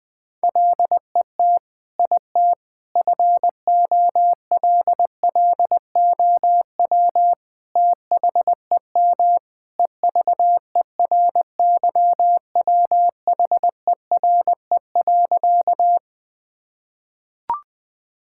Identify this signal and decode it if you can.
Morse code, heard as a keyed tone, about 21 wpm, 700 Hz. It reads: LET IT FOLLOW THEM EVERYWHERE.
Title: Sentences from Top 1600 Words 20wpm